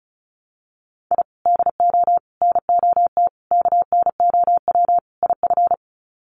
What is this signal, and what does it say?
Morse code, heard as a keyed tone, about 34 wpm, 700 Hz. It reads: I DO NOT KNOW IF
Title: Sets of 5 Words 35wpm